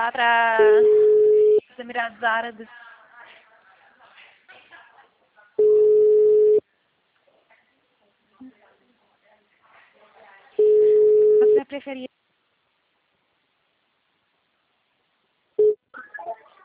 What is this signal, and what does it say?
Ringback tone (Europe, 425 Hz), with speech.